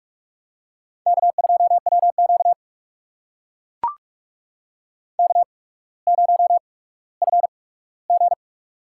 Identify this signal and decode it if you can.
Morse code, heard as a keyed tone, about 43 wpm, spaced as for 19 wpm, 700 Hz. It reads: K2WQ K0FG
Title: USA Call Signs - 8x Character Spacing - 45wpm